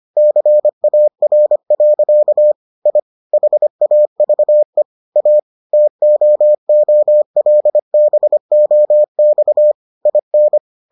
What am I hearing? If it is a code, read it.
Morse code, heard as a keyed tone, about 25 wpm, 600 Hz. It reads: CAR. I HAVE A TOOLBOX IN